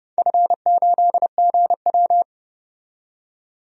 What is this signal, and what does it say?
Morse code, heard as a keyed tone, about 29 wpm, 700 Hz. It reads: F8GW
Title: DXCC-100 Call Signs - 4 Characters - 30wpm